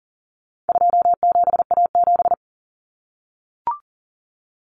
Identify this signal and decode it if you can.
Morse code, heard as a keyed tone, about 40 wpm, 700 Hz. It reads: 27A7
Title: Common Cut Numbers - 4-Digit 40wpm